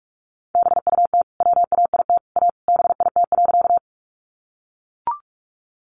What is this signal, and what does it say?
Morse code, heard as a keyed tone, about 44 wpm, 700 Hz. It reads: BUT WAIT A BIT.
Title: Sentences from Top 900 Words 45wpm